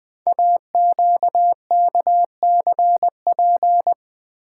Morse code, heard as a keyed tone, about 20 wpm, 700 Hz. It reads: AQKCP
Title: Code Groups - 5 Letters - 20wpm